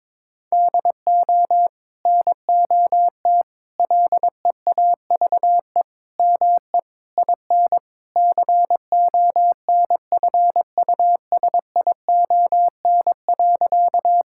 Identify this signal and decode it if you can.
Morse code, heard as a keyed tone, about 21 wpm, 700 Hz. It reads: DO NOT LEAVE ME IN CONFUSION.
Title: Sentences from Top 1600 Words 22wpm